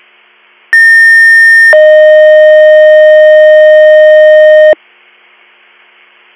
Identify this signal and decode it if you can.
Two-tone page, heard as a one-second tone, then a three-second tone, about 1800 then 620 Hz. It is heard from 0.7 to 4.7 s.